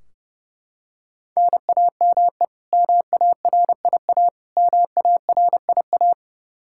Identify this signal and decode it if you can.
Morse code, heard as a keyed tone, about 30 wpm, 700 Hz. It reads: NAME MARIA MARIA
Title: Practice QSO - Exchange 1 - Beginning - 30wpm